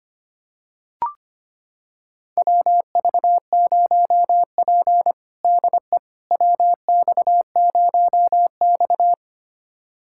Morse code, heard as a keyed tone, about 25 wpm, 700 Hz. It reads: WV0P DE WX0X
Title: USA Callsign DE Callsign - 4 Characters - 25wpm